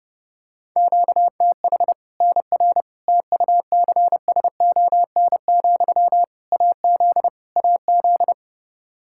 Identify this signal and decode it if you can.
Morse code, heard as a keyed tone, about 32 wpm, 700 Hz. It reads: QTH NR TUCSON, AZ AZ
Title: Practice QSO - Exchange 1 - Beginning - 30wpm